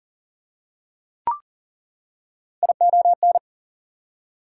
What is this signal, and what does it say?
Morse code, heard as a keyed tone, about 39 wpm, 700 Hz. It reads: ION
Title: Prefixes and Suffixes 40wpm